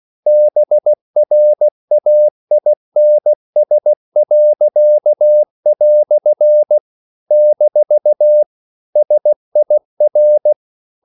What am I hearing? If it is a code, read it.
Morse code, heard as a keyed tone, about 16 wpm, 600 Hz. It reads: BRAINS." - SIR
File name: CW / Practice / EnglishQuotations0216